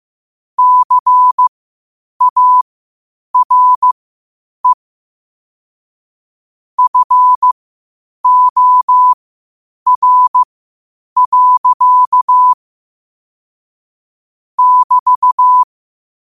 Morse code, heard as a keyed tone, about 15 wpm, spaced as for 8 wpm, 1000 Hz. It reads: CARE FOR. <BT>
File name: qotdfiles / 8wpm